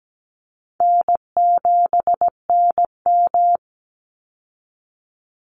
Morse code, heard as a keyed tone, about 17 wpm, 700 Hz. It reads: N7NM
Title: Easy Call Signs Repeated 3x - 17wpm